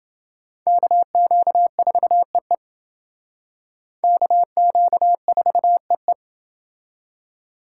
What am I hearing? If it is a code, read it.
Morse code, heard as a keyed tone, about 28 wpm, 700 Hz. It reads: KQ4EE KQ4EE